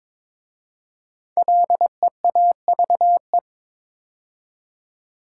Morse code, heard as a keyed tone, about 22 wpm, 700 Hz. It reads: LEAVE
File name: 5 Letter Words - First 3 Letters Given - 22wpm